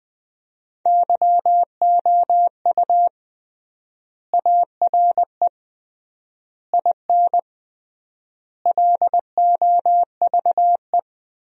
Morse code, heard as a keyed tone, about 20 wpm, 700 Hz. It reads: YOU ARE IN LOVE
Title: 3x Word-Spacing - Sentences from Top 400 Words - 20wpm